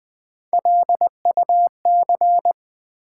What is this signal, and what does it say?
Morse code, heard as a keyed tone, about 20 wpm, 700 Hz. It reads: LUC
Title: Build A Word - 1 Letter At A Time - 3 to 5-Letter Words - 20wpm